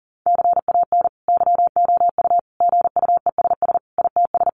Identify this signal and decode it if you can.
Morse code, heard as a keyed tone, about 40 wpm, 700 Hz. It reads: CAN YOU GUESS ITS
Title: Sentences from Top 1500 Words 40wpm